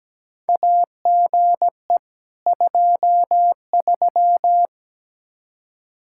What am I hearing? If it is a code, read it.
Morse code, heard as a keyed tone, about 17 wpm, 700 Hz. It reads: AGE 23
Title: Practice QSO - Exchange 3 - More Info - 17wpm